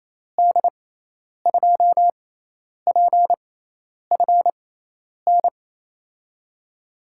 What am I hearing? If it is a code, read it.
Morse code, heard as a keyed tone, about 28 wpm, spaced as for 10 wpm, 700 Hz. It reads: D2PFN
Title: USA Call Signs - 6 Characters - 6x Character Spacing - 28wpm